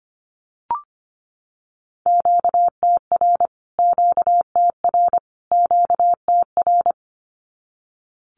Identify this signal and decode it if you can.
Morse code, heard as a keyed tone, about 25 wpm, 700 Hz. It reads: QTR QTR QTR
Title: Q-Codes Repeated 3x 25wpm